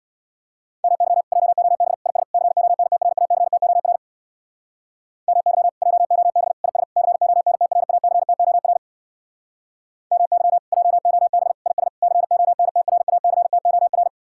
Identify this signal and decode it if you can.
Morse code, heard as a keyed tone, about 60 wpm, 700 Hz. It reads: MY JOB IS CONTRACTOR MY JOB IS CONTRACTOR MY JOB IS CONTRACTOR
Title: Speed-Racing Common QSO Phrases - 50wpm